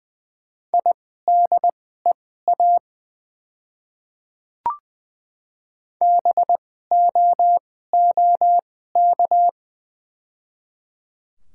Morse code, heard as a keyed tone, about 21 wpm, spaced as for 15 wpm, 700 Hz. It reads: IDEA BOOK
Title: 2x Character Spacing - Top 300 Words - 20wpm